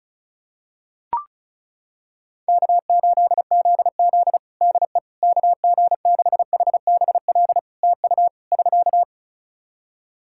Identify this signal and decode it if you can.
Morse code, heard as a keyed tone, about 36 wpm, 700 Hz. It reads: K8ZZ DE KG6HBL TU <SK>
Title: Practice QSO - Exchange 4 - Ending - 35wpm